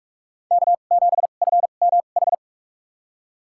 Morse code, heard as a keyed tone, about 47 wpm, spaced as for 33 wpm, 700 Hz. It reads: K7FMH
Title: USA Call Signs - 2x Character Spacing - 45wpm